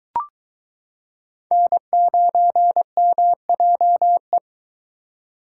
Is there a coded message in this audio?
Yes: Morse code from 1.5 to 4.4 s, heard as a keyed tone, about 22 wpm, 700 Hz. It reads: N9MJE